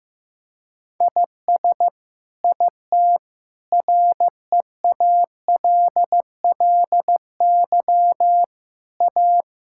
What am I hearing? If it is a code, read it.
Morse code, heard as a keyed tone, about 15 wpm, 700 Hz. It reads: IS IT REALLY A
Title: Sentences from Top 1000 Words - Review Part 5 - 15wpm